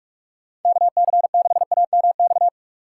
Morse code, heard as a keyed tone, about 43 wpm, 700 Hz. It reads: KC6AMX
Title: USA Call Signs - 6 Characters - 3x Character Spacing - 45wpm